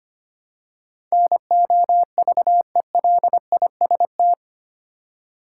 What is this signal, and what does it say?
Morse code, heard as a keyed tone, about 25 wpm, 700 Hz. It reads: NOVELIST
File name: 8 Letter Words - First 3 Letters Given - 25wpm